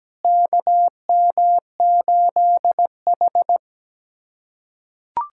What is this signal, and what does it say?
Morse code, heard as a keyed tone, about 17 wpm, 700 Hz. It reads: KM8H
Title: USA Callsign DE Callsign - 4 Characters - 17wpm